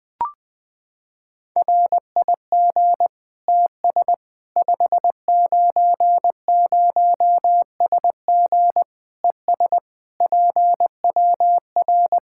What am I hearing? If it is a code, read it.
Morse code, heard as a keyed tone, about 20 wpm, 700 Hz. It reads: RIG TS 590SG ES PWR
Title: Practice QSO - Exchange 2 - Info - 20wpm